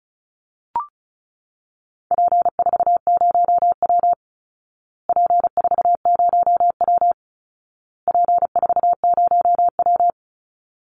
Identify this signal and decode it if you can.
Morse code, heard as a keyed tone, about 34 wpm, 700 Hz. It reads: P40W P40W P40W
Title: Call Signs Repeated 2x - 35wpm